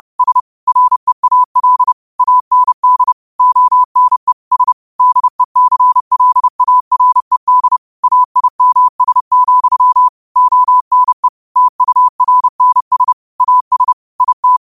Morse code, heard as a keyed tone, about 30 wpm, 1000 Hz. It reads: S REAL AND ONE S DECLARED AIMS, ONE TURNS AS IT
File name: qotdfiles / 30wpm